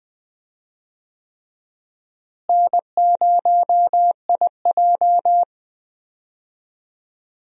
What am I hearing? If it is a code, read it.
Morse code, heard as a keyed tone, about 20 wpm, 700 Hz. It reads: N0IJ
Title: USA Call Signs - 4 Characters - Repeated 3x - 20wpm